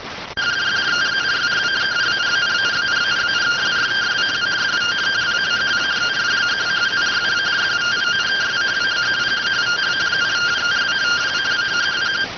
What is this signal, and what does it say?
Two-tone teleprinter signal, about 170 Hz shift.